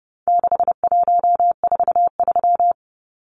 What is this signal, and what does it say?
Morse code, heard as a keyed tone, about 30 wpm, 700 Hz. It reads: 6143
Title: Numbers - 4-Digit 30wpm